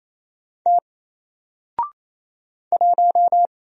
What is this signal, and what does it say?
Morse code, heard as a keyed tone, about 27 wpm, 700 Hz. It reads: T 1
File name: Single-Letter-Number-28wpm